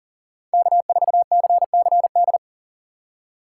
Morse code, heard as a keyed tone, about 37 wpm, 700 Hz. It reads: K4CCD
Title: USA Call Signs - 4x Character Spacing - 40wpm